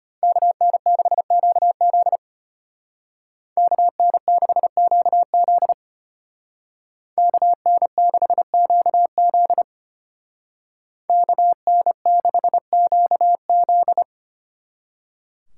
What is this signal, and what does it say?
Morse code, heard as a keyed tone, about 31 wpm, 700 Hz. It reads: KN6QZ KN6QZ KN6QZ KN6QZ